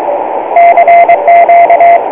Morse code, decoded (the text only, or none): CQ